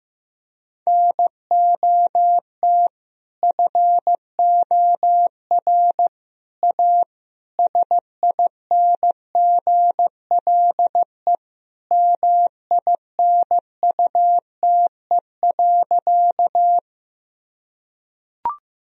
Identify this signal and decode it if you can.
Morse code, heard as a keyed tone, about 15 wpm, 700 Hz. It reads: NOT FOR A SINGLE MINUTE.
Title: Sentences from Top 1000 Words - Review Part 4 - 15wpm